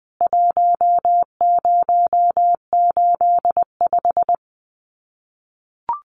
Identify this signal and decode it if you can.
Morse code, heard as a keyed tone, about 20 wpm, 700 Hz. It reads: 1085
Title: CWT Exchange 20wpm